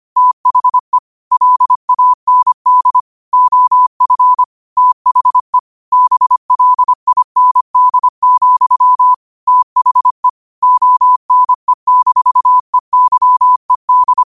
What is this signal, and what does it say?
Morse code, heard as a keyed tone, about 25 wpm, 1000 Hz. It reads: THE LAND OF THE BLIND, THE ONE-EYED